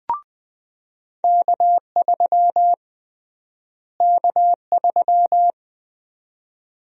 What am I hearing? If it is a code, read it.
Morse code, heard as a keyed tone, about 20 wpm, 700 Hz. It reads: K3 K3
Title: USA Call Sign Prefixes - Repeated 2x - 20wpm